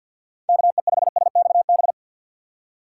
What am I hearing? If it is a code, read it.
Morse code, heard as a keyed tone, about 53 wpm, 700 Hz. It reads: KE5SXB
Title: USA Call Signs - 6 Characters - 4x Character Spacing - 50wpm